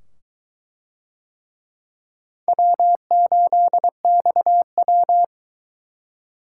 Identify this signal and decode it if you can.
Morse code, heard as a keyed tone, about 23 wpm, 700 Hz. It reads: W8XW